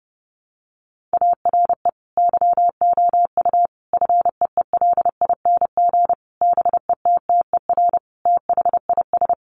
Morse code, heard as a keyed tone, about 30 wpm, 700 Hz. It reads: ARE YOU FEELING BETTER THIS
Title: Sentences from Top 1200 Words 30wpm